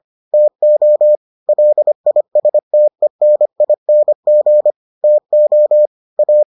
Morse code, heard as a keyed tone, about 25 wpm, 600 Hz. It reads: TO LISTENING TO A